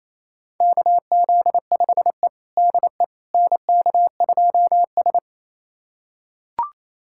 Morse code, heard as a keyed tone, about 27 wpm, 690 Hz. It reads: KZ5E DE NK2S
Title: USA Callsign DE Callsign - 4 Characters - 28wpm